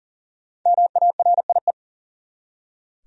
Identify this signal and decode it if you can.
Morse code, heard as a keyed tone, about 37 wpm, 700 Hz. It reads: MARIE
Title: USA Common Names - Max 5 Letters - 40wpm